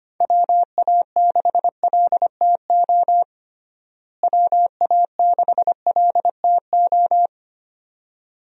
Morse code, heard as a keyed tone, about 25 wpm, 700 Hz. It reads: WA6LTO WA6LTO
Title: USA Call Signs - 6 Characters - Repeated 2x - 25wpm